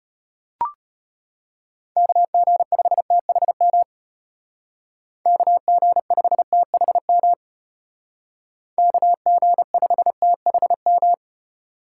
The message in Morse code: KG5THM KG5THM KG5THM

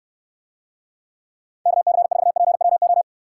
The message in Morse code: KC4PWO

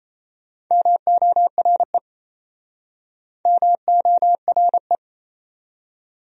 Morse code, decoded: MORE MORE

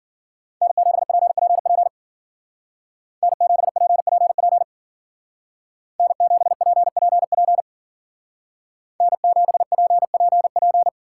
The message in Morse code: N7PPP N7PPP N7PPP N7PPP

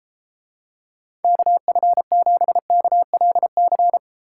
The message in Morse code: KF7KLC

